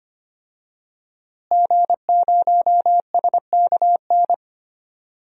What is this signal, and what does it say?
Morse code, heard as a keyed tone, about 25 wpm, 700 Hz. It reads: G0SKN
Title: DXCC-100 Call Signs - 5 Characters - Repeated 2x - 25wpm